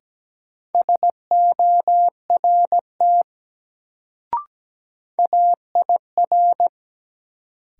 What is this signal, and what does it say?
Morse code, heard as a keyed tone, about 17 wpm, 700 Hz. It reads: SORT AIR
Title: Sets of 2 Words - New Word Drill - 17wpm